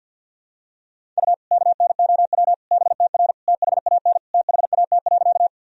Morse code, heard as a keyed tone, about 49 wpm, 700 Hz. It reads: U KNOW BTR THAN THAT.
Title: Sentences from Top 400 Words and Abbreviations 50wpm